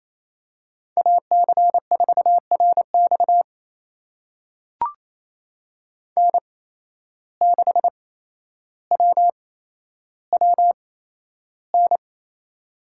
Morse code, heard as a keyed tone, about 28 wpm, spaced as for 12 wpm, 700 Hz. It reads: AC4RX N6WWN
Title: USA Call Signs - 5 Characters - 8x Character Spacing - 28wpm